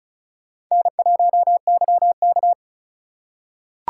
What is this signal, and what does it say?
Morse code, heard as a keyed tone, about 34 wpm, 700 Hz. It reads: N1YK
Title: USA Call Signs - 4 Characters - 4x Character Spacing - 35wpm